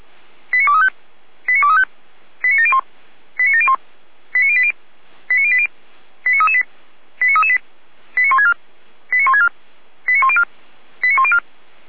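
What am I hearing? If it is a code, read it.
Five-tone selective call (ZVEI), heard as quick runs of five tones, tones of about 70 ms. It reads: 89235 89235 89701 89701 89080 89080 89338 89338 89164 89164 89104 89104
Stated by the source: Five-tone ZVEI selective call received on an SDR.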